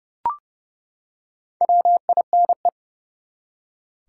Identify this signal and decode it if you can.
Morse code, heard as a keyed tone, about 29 wpm, 700 Hz. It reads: WINE
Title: Four Letter Words - 30wpm